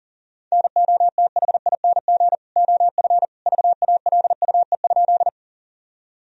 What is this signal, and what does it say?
Morse code, heard as a keyed tone, about 40 wpm, 700 Hz. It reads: NOTHING OF VALUE?
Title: Sentences from Top 1000 Words - Review Part 3 - 40wpm